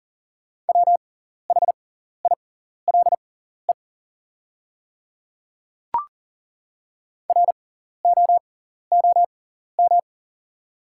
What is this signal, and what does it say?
Morse code, heard as a keyed tone, about 40 wpm, spaced as for 14 wpm, 700 Hz. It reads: WHILE ROOM
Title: 6x Character Spacing - Top 500 Words - 40wpm